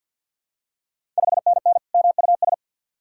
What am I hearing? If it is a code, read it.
Morse code, heard as a keyed tone, about 52 wpm, 690 Hz. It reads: 5NN MUS
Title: Ontario QSO Party Exchange 50wpm